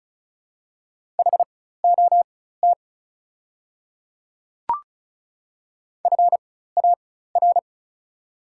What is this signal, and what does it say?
Morse code, heard as a keyed tone, about 36 wpm, spaced as for 16 wpm, 690 Hz. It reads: HOT FAR
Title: 4x Character Spacing - Top 300 Words - 35wpm